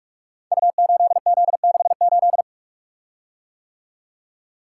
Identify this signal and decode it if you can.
Morse code, heard as a keyed tone, about 44 wpm, 700 Hz. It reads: U8Z68
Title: Code Groups - 5 Letter:Numbers - 45wpm